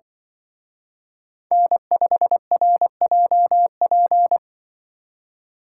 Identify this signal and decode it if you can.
Morse code, heard as a keyed tone, about 24 wpm, 700 Hz. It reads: N5RJP
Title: Speed Racing - USA 1x3 Call Signs 20wpm